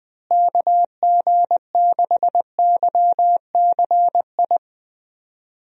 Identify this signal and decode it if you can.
Morse code, heard as a keyed tone, about 20 wpm, 700 Hz. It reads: KG6YCI
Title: USA Call Signs - 6 Characters - Repeated 3x - 20wpm